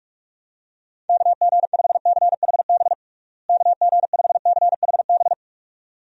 Morse code, heard as a keyed tone, about 46 wpm, 700 Hz. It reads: KG5CHB KG5CHB
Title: USA Call Signs - 6 Characters - Repeated 2x - 45wpm